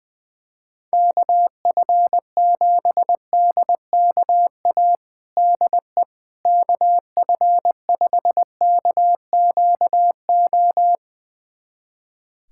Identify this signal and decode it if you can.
Morse code, heard as a keyed tone, about 20 wpm, 700 Hz. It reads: KF7DKA DE KF5KQO
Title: USA Callsign DE Callsign - 6 Characters - 20wpm